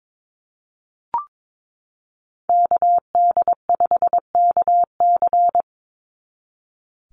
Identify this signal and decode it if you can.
Morse code, heard as a keyed tone, about 22 wpm, 700 Hz. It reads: KD5KC
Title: NA SOTA Chaser Callsigns 22wpm